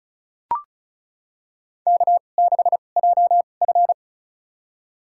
Morse code, heard as a keyed tone, about 36 wpm, spaced as for 25 wpm, 690 Hz. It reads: K6JF
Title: USA Call Signs - 4 Characters - 2x Character Spacing - 35wpm